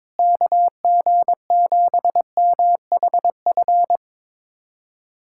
Morse code, heard as a keyed tone, about 22 wpm, 700 Hz. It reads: KG7MHF
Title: USA Callsign DE Callsign - 6 Characters - 22wpm